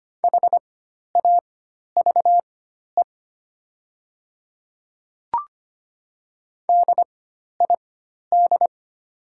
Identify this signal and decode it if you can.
Morse code, heard as a keyed tone, about 25 wpm, spaced as for 12 wpm, 700 Hz. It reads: HAVE DID
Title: 4x Character Spacing - Top 100 Words - 25wpm